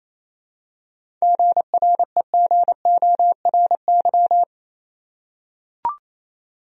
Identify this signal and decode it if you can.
Morse code, heard as a keyed tone, about 28 wpm, 700 Hz. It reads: GREGORY
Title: USA Common Names - 28wpm